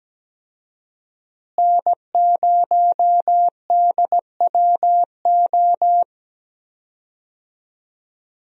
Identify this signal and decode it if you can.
Morse code, heard as a keyed tone, about 17 wpm, 700 Hz. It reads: N0DWO
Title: USA Call Signs - 5 Characters - Repeated 2x - 17wpm